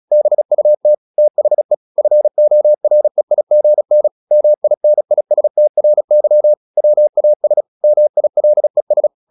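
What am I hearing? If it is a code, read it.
Morse code, heard as a keyed tone, about 36 wpm, 600 Hz. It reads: BUT THE FOREIGN MINISTRY WAS MILES
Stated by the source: Morse practice at 35 wpm from CW med Gnister, sending Ken Follett's novel Never in English.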